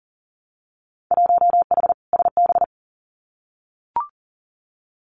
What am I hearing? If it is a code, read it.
Morse code, heard as a keyed tone, about 40 wpm, 700 Hz. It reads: 1H SB
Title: Winter Field Day Exchanges 40wpm